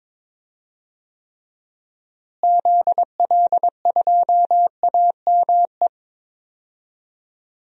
Morse code, heard as a keyed tone, about 22 wpm, 700 Hz. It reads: ZL2AME
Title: DXCC-100 Call Signs - 6 Characters - Repeated 2x - 22wpm